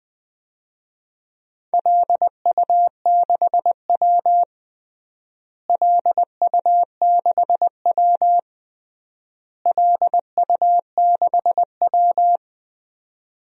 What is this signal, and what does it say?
Morse code, heard as a keyed tone, about 20 wpm, 700 Hz. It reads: LU6W LU6W LU6W
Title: DXCC-100 Call Signs - 4 Characters - Repeated 3x - 20wpm